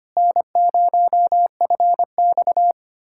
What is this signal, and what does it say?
Morse code, heard as a keyed tone, about 25 wpm, 700 Hz. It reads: N0FX
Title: USA Callsign DE Callsign - 4 Characters - 25wpm